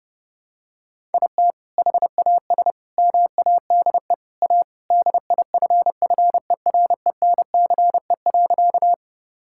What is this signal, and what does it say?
Morse code, heard as a keyed tone, about 30 wpm, 700 Hz. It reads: IT HAS MADE A DIFFERENCE.
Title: Sentences from Top 1000 Words - Review Part 5 - 30wpm